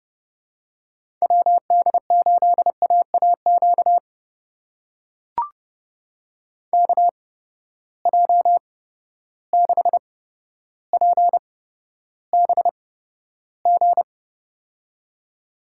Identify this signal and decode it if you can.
Morse code, heard as a keyed tone, about 32 wpm, spaced as for 13 wpm, 700 Hz. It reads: WD8AAQ KJ6PBG
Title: USA Call Signs - 6 Characters - 8x Character Spacing - 30wpm